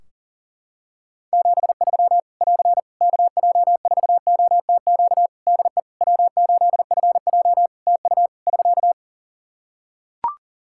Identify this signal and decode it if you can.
Morse code, heard as a keyed tone, about 37 wpm, 700 Hz. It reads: 73 <AR> KJ4OTQ DE W8FJ TU <SK>
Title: Practice QSO - Exchange 4 - Ending - 40wpm